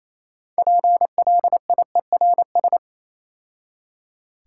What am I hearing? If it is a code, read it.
Morse code, heard as a keyed tone, about 28 wpm, 690 Hz. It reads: PLIERS